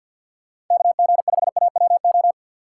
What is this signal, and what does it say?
Morse code, heard as a keyed tone, about 53 wpm, 700 Hz. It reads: KG5AWO
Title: USA Call Signs - 6 Characters - 3x Character Spacing - 50wpm